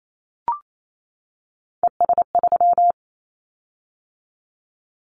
Morse code, heard as a keyed tone, about 28 wpm, 700 Hz. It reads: ES3
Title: Callsign Prefixes for Top 100 DXCC - 28wpm